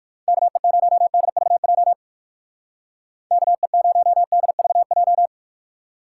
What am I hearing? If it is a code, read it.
Morse code, heard as a keyed tone, about 59 wpm, 700 Hz. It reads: KE0DVJ KE0DVJ